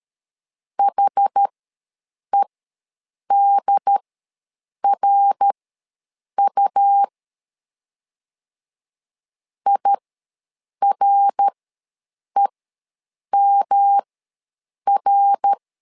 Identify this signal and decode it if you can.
Morse code, heard as a keyed tone, about 13 wpm, spaced as for 7 wpm, 800 Hz. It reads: HEDRU IREMR
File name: LEZIONI / 3 MWRO / 3 MWR ws 12wpm 1809